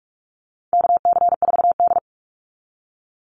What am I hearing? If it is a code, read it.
Morse code, heard as a keyed tone, about 44 wpm, 700 Hz. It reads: KC4D
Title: Call Signs Repeated 3x - 45wpm